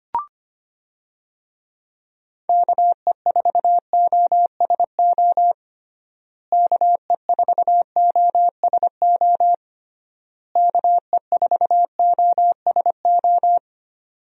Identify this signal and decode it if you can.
Morse code, heard as a keyed tone, about 25 wpm, 700 Hz. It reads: KE4OSO KE4OSO KE4OSO
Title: USA Call Signs - 6 Characters - Repeated 3x - 25wpm